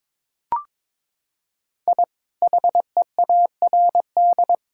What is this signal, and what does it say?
Morse code, heard as a keyed tone, about 21 wpm, 700 Hz. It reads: I HEARD
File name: practice / 22wpm / Sentences with 2-Word Phrases - 22wpm